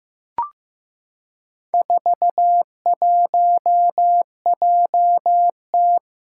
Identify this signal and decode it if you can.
Morse code, heard as a keyed tone, about 15 wpm, 700 Hz. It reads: 41JT
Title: Four Challenging Letter-Number Combinations 15wpm